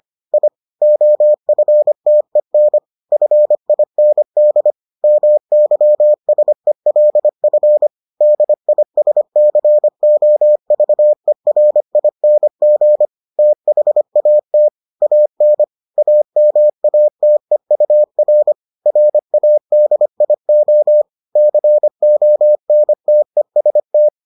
Morse code, heard as a keyed tone, about 25 wpm, 600 Hz. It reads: I OFTEN FIND MYSELF DISCOVERING THAT AN AMATEUR RADIO CONTEST